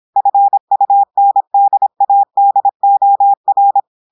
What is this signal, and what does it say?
Morse code, heard as a keyed tone, about 26 wpm, 800 Hz. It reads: FUNDADOR
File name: audios / portugais / portugais texte en clair 1 vitesse 26 mots minutes